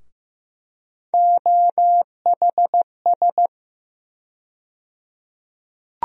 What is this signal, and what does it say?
Morse code, heard as a keyed tone, about 15 wpm, 700 Hz. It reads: OHS
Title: USA Call Sign Postfixes - 3 Letters 15wpm